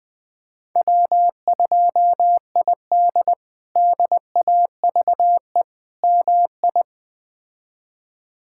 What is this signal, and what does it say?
Morse code, heard as a keyed tone, about 21 wpm, 700 Hz. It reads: W2ID DAVE MI
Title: K1USN SST QSO Practice - 20wpm